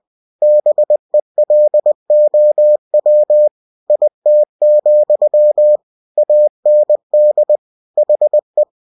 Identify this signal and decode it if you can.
Morse code, heard as a keyed tone, about 20 wpm, 600 Hz. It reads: BELOW IT, AND HE
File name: cw-books / winnie-the-pooh / winnie-20wpm-page-0031